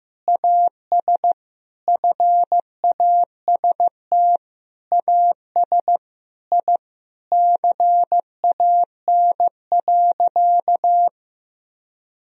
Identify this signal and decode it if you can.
Morse code, heard as a keyed tone, about 15 wpm, 700 Hz. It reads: AS FAST AS I CAN.
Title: Sentences from Top 1000 Words - Review Part 4 - 15wpm